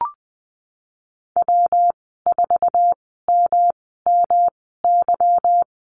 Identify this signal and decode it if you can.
Morse code, heard as a keyed tone, about 20 wpm, spaced as for 15 wpm, 700 Hz. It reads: W4MMY
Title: 161 Lesson 26g - Letter g - Call Signs 20wpm